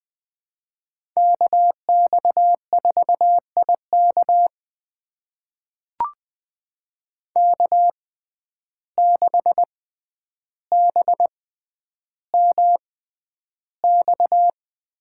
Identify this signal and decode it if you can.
Morse code, heard as a keyed tone, about 21 wpm, spaced as for 10 wpm, 700 Hz. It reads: KX4IK K6BMX
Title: USA Call Signs - 5 Characters - 6x Character Spacing - 20wpm